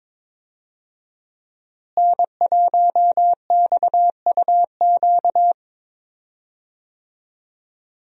Morse code, heard as a keyed tone, about 22 wpm, 700 Hz. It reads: N1XUQ
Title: USA Call Signs - 5 Characters - Repeated 3x - 22wpm